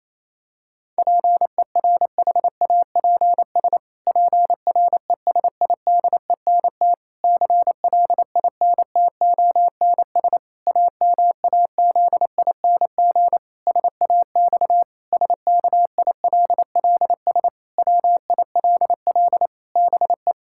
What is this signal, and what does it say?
Morse code, heard as a keyed tone, about 28 wpm, 700 Hz. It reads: PERHAPS PRESIDENT CLINTONS AMAZING SAX SKILLS WILL BE
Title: Pangram Challenge v2 - 28wpm